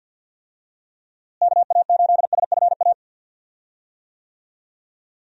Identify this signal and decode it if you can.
Morse code, heard as a keyed tone, about 49 wpm, 700 Hz. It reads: KA8SFA
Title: USA Call Signs - 6 Characters - Repeated 2x - 50wpm